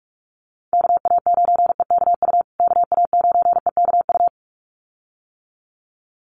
Morse code, heard as a keyed tone, about 44 wpm, 700 Hz. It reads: KA9EKU KA9EKU
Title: USA Call Signs Repeated 2x 45wpm